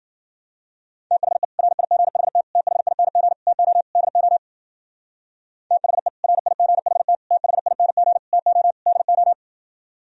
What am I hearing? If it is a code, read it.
Morse code, heard as a keyed tone, about 49 wpm, 700 Hz. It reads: THE RIGHT THING TO DO THE RIGHT THING TO DO
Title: Speed-Racing - Sets of 5 Words 40wpm